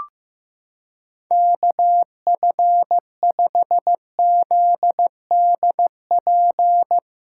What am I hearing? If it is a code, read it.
Morse code, heard as a keyed tone, about 15 wpm, 700 Hz. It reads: KF5ZDP
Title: USA Callsign DE Callsign 15wpm